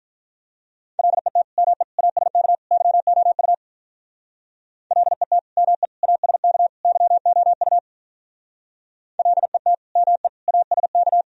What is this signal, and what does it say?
Morse code, heard as a keyed tone, about 46 wpm, 700 Hz. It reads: LET ME ASK YOU LET ME ASK YOU LET ME ASK